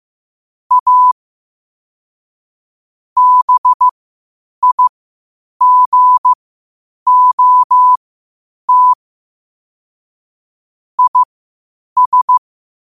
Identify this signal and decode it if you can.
Morse code, heard as a keyed tone, about 15 wpm, spaced as for 8 wpm, 1000 Hz. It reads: A BIGOT IS
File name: qotdfiles / 8wpm